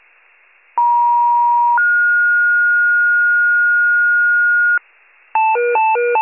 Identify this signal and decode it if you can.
Two-tone page, heard as a one-second tone, then a three-second tone, about 950 then 1430 Hz, with noise. It is heard from 0.8 to 4.8 s.